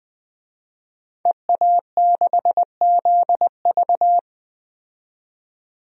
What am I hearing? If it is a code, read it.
Morse code, heard as a keyed tone, about 21 wpm, 700 Hz. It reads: EA6ZV
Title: DXCC-100 Call Signs - 5 Characters - 20wpm